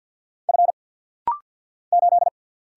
Morse code, heard as a keyed tone, about 50 wpm, 700 Hz. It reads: F 8
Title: Single Letter-Number - Rapid-Fire 50wpm